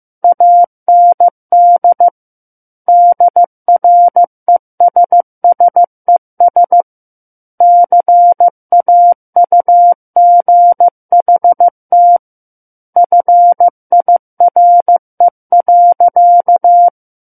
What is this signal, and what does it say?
Morse code, heard as a keyed tone, about 15 wpm, 700 Hz. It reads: AND DRESSES CAUGHT FIRE.
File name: Book One / 06-Chapter 6 - 15 WPM